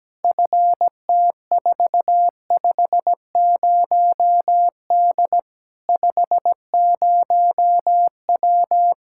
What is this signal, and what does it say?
Morse code, heard as a keyed tone, about 17 wpm, 700 Hz. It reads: FT450D 50W
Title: Practice QSO - Exchange 2 - Info - 17wpm